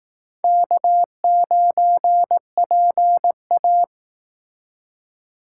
Morse code, heard as a keyed tone, about 18 wpm, 700 Hz. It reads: K9PA
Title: Speed Racing - USA 1x2 Call Signs 15wpm